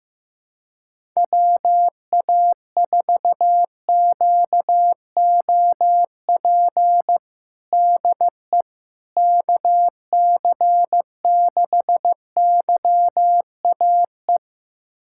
Morse code, heard as a keyed tone, about 15 wpm, 700 Hz. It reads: WA4QOP DE KC6YAE
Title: USA Callsign DE Callsign 15wpm